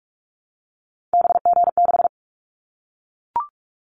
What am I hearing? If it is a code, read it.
Morse code, heard as a keyed tone, about 44 wpm, 700 Hz. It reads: BG6
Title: Callsign Prefixes for Top 100 DXCC - 45wpm